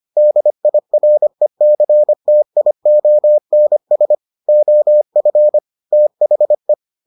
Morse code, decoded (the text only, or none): DIRECTIONS OF THE